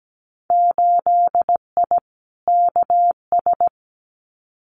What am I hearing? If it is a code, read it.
Morse code, heard as a keyed tone, about 17 wpm, 700 Hz. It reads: 8I KS
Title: Winter Field Day Exchanges Encore 17wpm